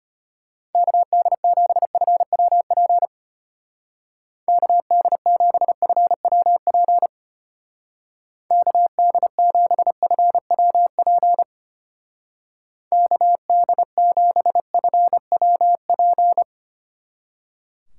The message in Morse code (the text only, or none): KD7FWP KD7FWP KD7FWP KD7FWP